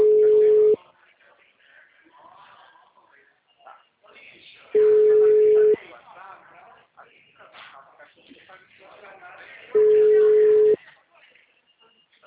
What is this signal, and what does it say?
Ringback tone (Europe, 425 Hz), with speech.